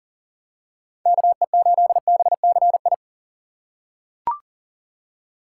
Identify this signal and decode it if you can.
Morse code, heard as a keyed tone, about 40 wpm, 700 Hz. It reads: KE8BCI
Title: USA Call Signs - 6 Characters - 8x Character Spacing - 40wpm